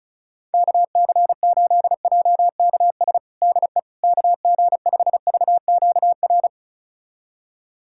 Morse code, heard as a keyed tone, about 36 wpm, 700 Hz. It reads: KC8JKS DE KG5VQR
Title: USA Callsign DE Callsign - 6 Characters - 35wpm